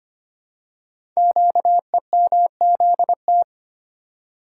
Morse code, heard as a keyed tone, about 25 wpm, 700 Hz. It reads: QEMZT
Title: Code Groups - 5 Letters - 25wpm